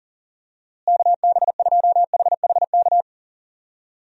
Morse code, heard as a keyed tone, about 38 wpm, 700 Hz. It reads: KB2HHK
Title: USA Call Signs - 6 Characters - Repeated 2x - 40wpm